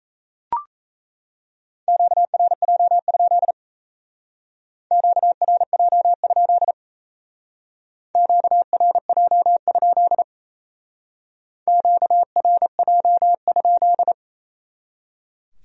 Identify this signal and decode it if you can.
Morse code, heard as a keyed tone, about 34 wpm, 700 Hz. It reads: QRJ? QRJ? QRJ? QRJ?
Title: Speed-Racing Q-Codes - 28wpm